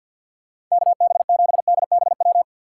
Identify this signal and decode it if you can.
Morse code, heard as a keyed tone, about 50 wpm, 700 Hz. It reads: KB7DBW
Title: USA Call Signs - 6 Characters - 4x Character Spacing - 50wpm